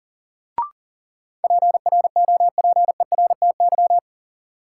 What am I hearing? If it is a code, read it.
Morse code, heard as a keyed tone, about 40 wpm, 700 Hz. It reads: PROPERTY
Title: Sentences from Top 1000 Words - New Word Drill - 40wpm